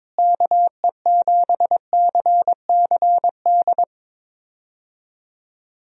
Morse code, heard as a keyed tone, about 22 wpm, 700 Hz. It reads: KE7CCD
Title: USA Call Signs - 6 Characters - Repeated 2x - 22wpm